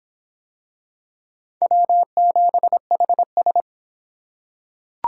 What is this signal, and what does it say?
Morse code, heard as a keyed tone, about 24 wpm, 700 Hz. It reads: W7HS